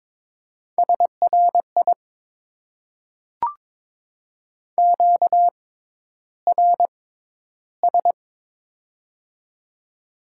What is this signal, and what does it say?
Morse code, heard as a keyed tone, about 22 wpm, spaced as for 11 wpm, 700 Hz. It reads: SRI QRS
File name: Top 90 QSO Elements - 6x Character Spacing - 22wpm